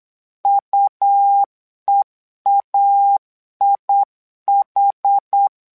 Morse code, heard as a keyed tone, about 8 wpm, 800 Hz. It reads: UEAIH
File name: LEZIONI / 2 UHDN / 2 UH 8wpm 2129